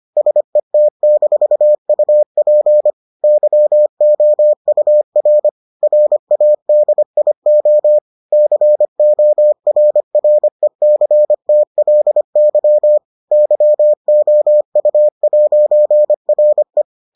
Morse code, heard as a keyed tone, about 25 wpm, 600 Hz. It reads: SET-UP YOUR RADIO CORRECTLY YOU'RE